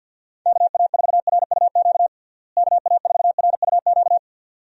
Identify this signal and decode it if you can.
Morse code, heard as a keyed tone, about 50 wpm, 700 Hz. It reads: KA4RUQ KA4RUQ
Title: USA Call Signs - 6 Characters - Repeated 2x - 50wpm